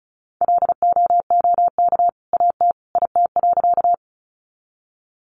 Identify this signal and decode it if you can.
Morse code, heard as a keyed tone, about 34 wpm, 700 Hz. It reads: LOOK AT IT.
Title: Sentences from Top 100 Words 35wpm